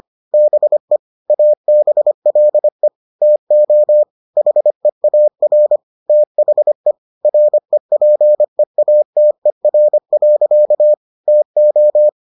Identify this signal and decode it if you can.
Morse code, heard as a keyed tone, about 25 wpm, 600 Hz. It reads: BE ABLE TO HEAR THE REPEATER. TO